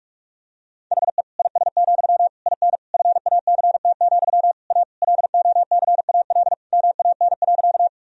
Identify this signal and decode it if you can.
Morse code, heard as a keyed tone, about 47 wpm, 700 Hz. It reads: HE IS, IN FACT, A LOCAL MAN.
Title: Sentences from Top 1000 Words - Review Part 3 - 45wpm